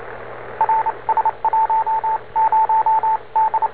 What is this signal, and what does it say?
Morse code, heard as a keyed tone, about 29 wpm, spaced as for 25 wpm, 870 Hz, with radio noise. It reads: RS10 D
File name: n4st / rs10 robot aa3o